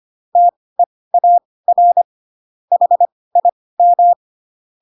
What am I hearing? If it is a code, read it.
Morse code, heard as a keyed tone, about 25 wpm, spaced as for 18 wpm, 700 Hz. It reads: TEAR HIM
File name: Pinocchio ch3 12 25 ik1hge